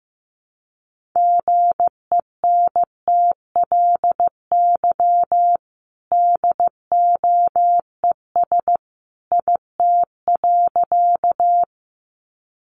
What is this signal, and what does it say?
Morse code, heard as a keyed tone, about 15 wpm, 700 Hz. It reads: GENTLY DOES IT.
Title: Sentences from Top 1500 Words 15wpm